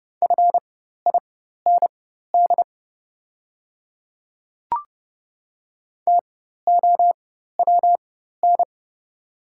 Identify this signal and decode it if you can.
Morse code, heard as a keyed tone, about 30 wpm, spaced as for 14 wpm, 690 Hz. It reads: FIND TOWN
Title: 4x Character Spacing - Top 500 Words - 30wpm